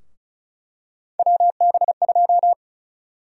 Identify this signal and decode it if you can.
Morse code, heard as a keyed tone, about 34 wpm, 700 Hz. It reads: WB2
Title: USA Call Sign Prefixes - 35wpm